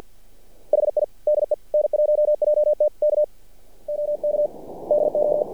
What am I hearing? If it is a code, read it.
Morse code, heard as a keyed tone, about 53 wpm, 600 Hz. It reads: HI DE N1JT K OK DO
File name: archived / CRSnet